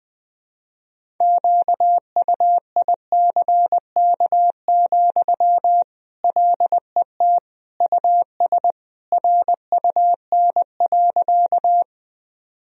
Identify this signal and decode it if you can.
Morse code, heard as a keyed tone, about 21 wpm, 700 Hz. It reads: QUICK, LET US RUN.
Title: Sentences from Top 1000 Words - Review Part 1 - 20wpm